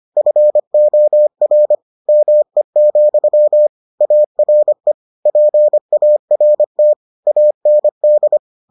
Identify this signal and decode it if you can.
Morse code, heard as a keyed tone, about 25 wpm, 600 Hz. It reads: FOR ME, ARE PART AND